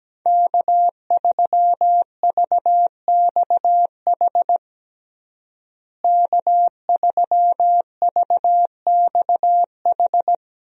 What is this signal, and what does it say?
Morse code, heard as a keyed tone, about 17 wpm, 700 Hz. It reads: K3VXH K3VXH
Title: USA Call Signs - 5 Characters - Repeated 2x - 17wpm